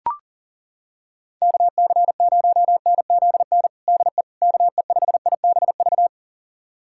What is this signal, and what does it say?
Morse code, heard as a keyed tone, about 40 wpm, 700 Hz. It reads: KC0NZN DE KE5IBV
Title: USA Callsign DE Callsign 40wpm